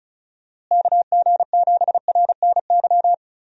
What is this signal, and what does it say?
Morse code, heard as a keyed tone, about 34 wpm, 700 Hz. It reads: KG7RNY
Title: USA Call Signs - 6 Characters - Repeated 3x - 35wpm